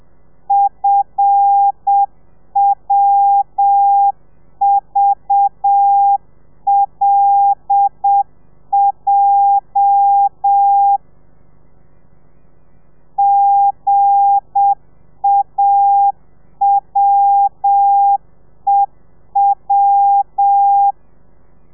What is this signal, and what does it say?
Morse code, heard as a keyed tone, about 7 wpm, 800 Hz. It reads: FWVLJ GAWEW
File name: telegrafia / i2tfj / Lez8 30